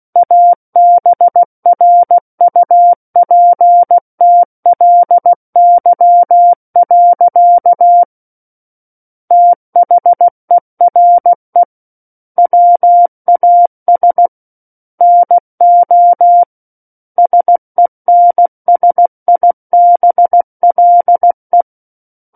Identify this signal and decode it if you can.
Morse code, heard as a keyed tone, about 16 wpm, 700 Hz. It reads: ABRUPTLY. THERE WAS NO SENSIBLE
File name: Book One / 07-Chapter 7 - 16 WPM